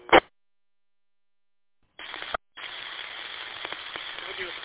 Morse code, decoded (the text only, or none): none